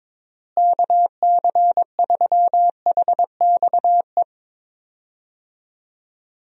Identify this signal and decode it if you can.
Morse code, heard as a keyed tone, about 22 wpm, 700 Hz. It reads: KC3HXE
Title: USA Call Signs - 6 Characters - Repeated 3x - 22wpm